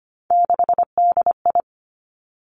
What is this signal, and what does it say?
Morse code, heard as a keyed tone, about 25 wpm, 700 Hz. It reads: 6DI
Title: Three Challenging Letter-Number Combinations 25wpm